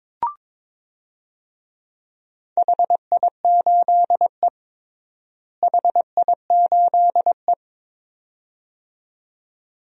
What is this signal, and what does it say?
Morse code, heard as a keyed tone, about 22 wpm, 700 Hz. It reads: HI8E HI8E
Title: DXCC-100 Call Signs - 4 Characters - Repeated 2x - 22wpm